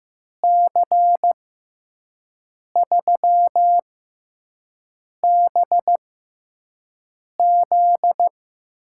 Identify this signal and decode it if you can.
Morse code, heard as a keyed tone, about 16 wpm, spaced as for 5 wpm, 700 Hz. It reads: C3BZ
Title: USA Call Signs - 6 Characters - 6x Character Spacing - 15wpm